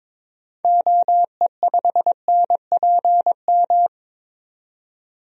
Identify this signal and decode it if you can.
Morse code, heard as a keyed tone, about 22 wpm, 700 Hz. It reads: OE5NPM
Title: DXCC-100 Call Signs - 6 Characters - 22wpm